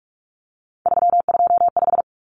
Morse code, heard as a keyed tone, about 44 wpm, 700 Hz. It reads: 325
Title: Numbers - 3-Digit 45wpm